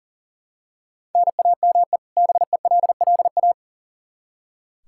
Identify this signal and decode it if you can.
Morse code, heard as a keyed tone, about 40 wpm, 700 Hz. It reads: NAME BELLA
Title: Practice QSO - Exchange 1 - Beginning - 40wpm